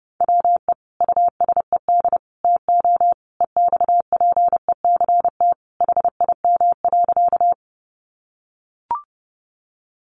Morse code, heard as a keyed tone, about 30 wpm, 700 Hz. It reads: WE USED TO EXPECT HIM.
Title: Sentences from Top 1100 Words 30wpm